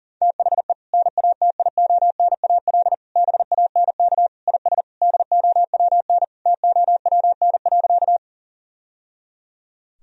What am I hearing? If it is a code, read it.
Morse code, heard as a keyed tone, about 40 wpm, 700 Hz. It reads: THE NATIONAL BANK IS DOWN TOWN.
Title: Sentences from Top 1000 Words - Review Part 4 - 40wpm